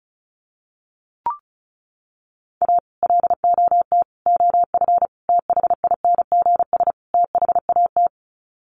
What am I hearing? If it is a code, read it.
Morse code, heard as a keyed tone, about 34 wpm, 700 Hz. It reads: A LOT OF THINGS THAT
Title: Sets of 5 Words 35wpm